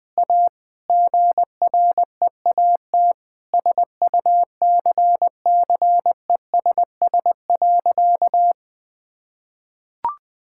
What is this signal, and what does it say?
Morse code, heard as a keyed tone, about 20 wpm, 700 Hz. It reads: A GREAT SUCCESS.
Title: Sentences with 6-Word Phrases - 20wpm